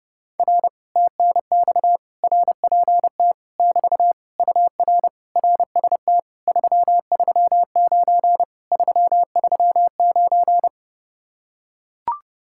Morse code, heard as a keyed tone, about 30 wpm, 700 Hz. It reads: R TNX RPT <BT> UR RST 339 339